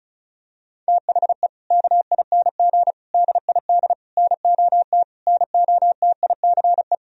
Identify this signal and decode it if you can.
Morse code, heard as a keyed tone, about 36 wpm, 700 Hz. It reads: THE KING DID NOT NOTICE
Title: Sentences from Top 1000 Words - Review Part 2 - 35wpm